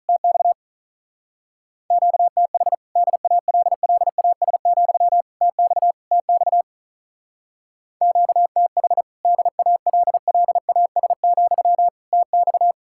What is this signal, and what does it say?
Morse code, heard as a keyed tone, about 45 wpm, 700 Hz. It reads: TX QTH DALLAS, TX TX QTH DALLAS, TX